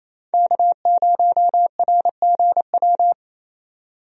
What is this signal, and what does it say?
Morse code, heard as a keyed tone, about 28 wpm, 700 Hz. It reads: K0RGW
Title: USA Call Signs - 5 Characters - 28wpm